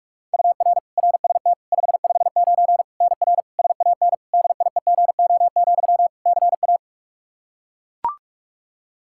Morse code, heard as a keyed tone, about 41 wpm, 700 Hz. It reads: UR RST 559 NR SAN DIEGO, CA
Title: Practice QSO - Exchange 1 - Beginning - 45wpm